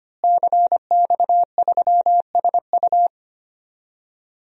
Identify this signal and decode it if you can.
Morse code, heard as a keyed tone, about 25 wpm, 700 Hz. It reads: CX3SU
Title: DXCC-100 Call Signs - 5 Characters - Repeated 3x - 25wpm